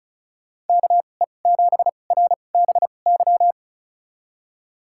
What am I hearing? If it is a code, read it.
Morse code, heard as a keyed tone, about 36 wpm, spaced as for 25 wpm, 700 Hz. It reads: KE7RBY
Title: USA Call Signs - 6 Characters - 2x Character Spacing - 35wpm